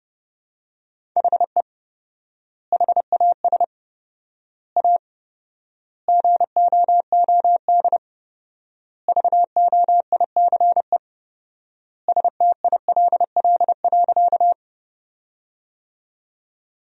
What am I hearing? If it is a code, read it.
Morse code, heard as a keyed tone, about 30 wpm, 700 Hz. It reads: HE HAS A GOOD VOICE STILL.
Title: 4x Word-Spacing - Sentences from Top 500 Words - 30wpm